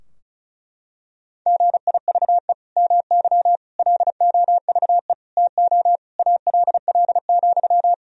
Morse code, heard as a keyed tone, about 36 wpm, 700 Hz. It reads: GIVE MY LOVE TO ALL,
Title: Sentences from Top 1000 Words - Review Part 4 - 35wpm